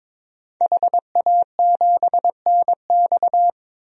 Morse code, heard as a keyed tone, about 21 wpm, 700 Hz. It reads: HA7NX
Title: DXCC-100 Call Signs - 5 Characters - 22wpm